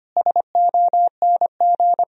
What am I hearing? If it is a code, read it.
Morse code, heard as a keyed tone, about 25 wpm, 700 Hz. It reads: SONG
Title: Sentences from Top 1000 Words - Review Part 2 - 25wpm